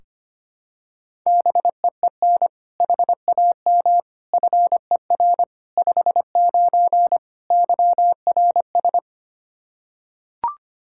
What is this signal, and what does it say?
Morse code, heard as a keyed tone, about 25 wpm, 700 Hz. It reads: BEEN HAM FER 59 YRS
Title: Practice QSO - Exchange 3 - More Info - 25wpm